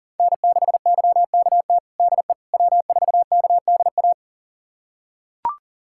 Morse code, heard as a keyed tone, about 40 wpm, 700 Hz. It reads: N6YKT DE W4KDA
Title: USA Callsign DE Callsign - 5 Characters - 40wpm